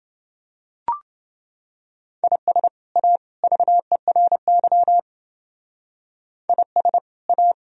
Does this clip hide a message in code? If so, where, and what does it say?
Yes: Morse code from 2.2 to 7.5 s, heard as a keyed tone, about 28 wpm, 700 Hz. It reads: IS A VERY IS A